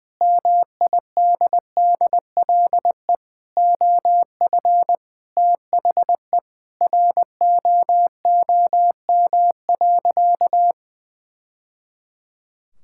Morse code, heard as a keyed tone, about 20 wpm, 700 Hz. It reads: MIDDLE OF THE ROOM.
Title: Sentences from Top 1000 Words - Review Part 6 - 20wpm